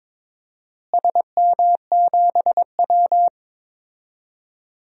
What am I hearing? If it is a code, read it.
Morse code, heard as a keyed tone, about 22 wpm, 700 Hz. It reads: SM7W
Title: DXCC-100 Call Signs - 4 Characters - Repeated 3x - 22wpm